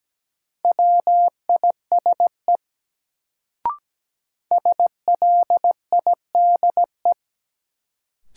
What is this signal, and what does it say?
Morse code, heard as a keyed tone, about 17 wpm, 700 Hz. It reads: WISE SLIDE
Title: Sentences from Top 1600 Words - New Word Drill - 17wpm